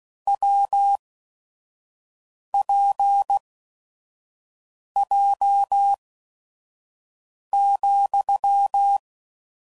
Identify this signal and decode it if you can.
Morse code, heard as a keyed tone, about 16 wpm, spaced as for 5 wpm, 800 Hz. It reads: WPJ,
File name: Combinations / Difficult Combos / AWJ1